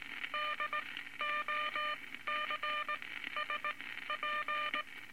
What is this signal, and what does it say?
Morse code, heard as a keyed tone, about 19 wpm, spaced as for 13 wpm, 1200 Hz, with noise. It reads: DOCSP
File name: Morse Code Records / 19 A LCM 166 Ten Words Per Minute